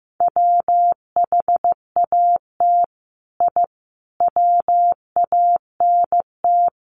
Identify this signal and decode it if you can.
Morse code, heard as a keyed tone, about 15 wpm, 700 Hz. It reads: WHAT I WANT
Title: Sets of 5 Words Encore 15wpm